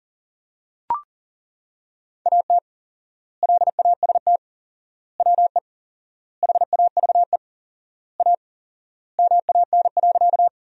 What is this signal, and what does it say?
Morse code, heard as a keyed tone, about 40 wpm, 700 Hz. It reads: AT LAST WE HAVE A MAN.
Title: 4x Word-Spacing - Sentences from Top 300 Words - 40wpm